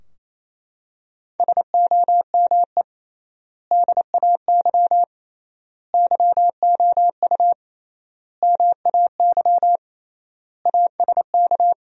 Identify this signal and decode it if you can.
Morse code, heard as a keyed tone, about 28 wpm, 700 Hz. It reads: SOME DAY YOU MAY ASK
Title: 3x Word-Spacing - Sentences from Top 200 Words - 28wpm